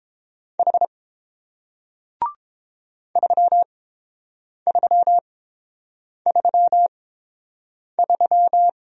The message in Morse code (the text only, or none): H 3333